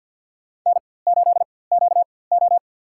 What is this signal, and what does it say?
Morse code, heard as a keyed tone, about 53 wpm, spaced as for 24 wpm, 700 Hz. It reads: N8QO
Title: USA Call Signs - 4x Character Spacing - 50wpm